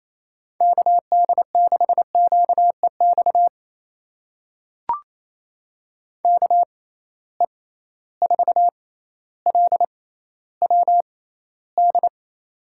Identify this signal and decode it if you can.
Morse code, heard as a keyed tone, about 28 wpm, spaced as for 14 wpm, 700 Hz. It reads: KD6QEX KE4LWD
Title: USA Call Signs - 6 Characters - 6x Character Spacing - 28wpm